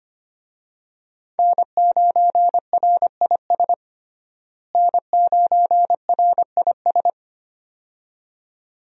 Morse code, heard as a keyed tone, about 25 wpm, 700 Hz. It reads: N9RIS N9RIS
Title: USA Call Signs - 5 Characters - Repeated 2x - 25wpm